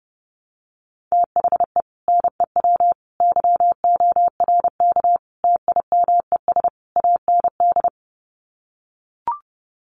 Morse code, heard as a keyed tone, about 30 wpm, 700 Hz. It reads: THE NEW YORK TIMES AND
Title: Sets of 5 Words Encore 30wpm